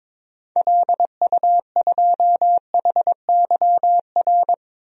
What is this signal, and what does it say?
Morse code, heard as a keyed tone, about 22 wpm, 700 Hz. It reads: LU2HYR
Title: DXCC-100 Call Signs - 6 Characters - Repeated 3x - 22wpm